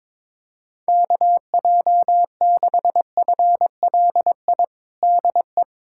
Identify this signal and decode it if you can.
Morse code, heard as a keyed tone, about 22 wpm, 700 Hz. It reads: KJ6FLI DE
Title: USA Callsign DE Callsign - 6 Characters - 22wpm